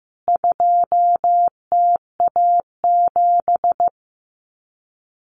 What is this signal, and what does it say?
Morse code, heard as a keyed tone, about 15 wpm, 700 Hz. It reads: 2TA7
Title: CWT Exchange 15wpm